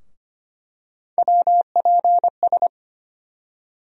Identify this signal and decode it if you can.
Morse code, heard as a keyed tone, about 25 wpm, 700 Hz. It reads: WPS
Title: USA Call Sign Postfixes - 3 Letters 25wpm